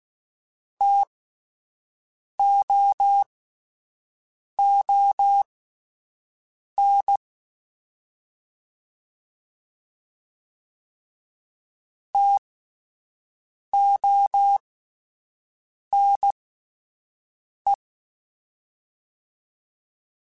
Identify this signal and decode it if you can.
Morse code, heard as a keyed tone, about 16 wpm, spaced as for 5 wpm, 800 Hz. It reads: TOON TONE
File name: Spaced Alphas / Spaced Alphas mp3s / Ts sp @10